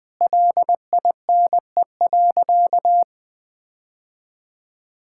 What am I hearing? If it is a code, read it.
Morse code, heard as a keyed tone, about 20 wpm, 700 Hz. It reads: LINE.
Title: All-Age Appropriate Jokes - 20wpm